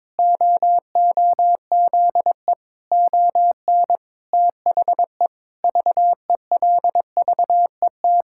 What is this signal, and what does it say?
Morse code, heard as a keyed tone, about 22 wpm, 700 Hz. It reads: OOZE ON THE VELVET
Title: Pangram Challenge v2 - 22wpm